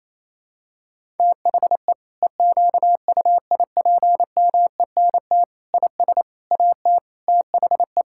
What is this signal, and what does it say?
Morse code, heard as a keyed tone, about 28 wpm, 700 Hz. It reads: THE EQUIPMENT IS AT THE
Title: Sentences from Top 1000 Words - Review Part 5 - 28wpm